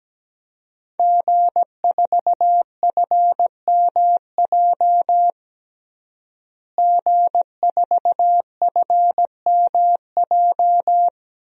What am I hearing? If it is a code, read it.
Morse code, heard as a keyed tone, about 17 wpm, 700 Hz. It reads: G4FMJ G4FMJ
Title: DXCC-100 Call Signs - 5 Characters - Repeated 3x - 17wpm